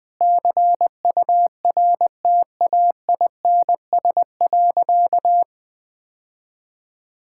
Morse code, heard as a keyed tone, about 20 wpm, 700 Hz. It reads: CURTAINS.
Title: Easy Sentences with Hints - 20wpm